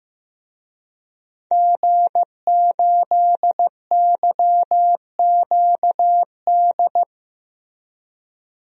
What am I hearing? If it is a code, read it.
Morse code, heard as a keyed tone, about 15 wpm, 700 Hz. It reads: G8YQD
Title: DXCC-100 Call Signs - 5 Characters - Repeated 2x - 15wpm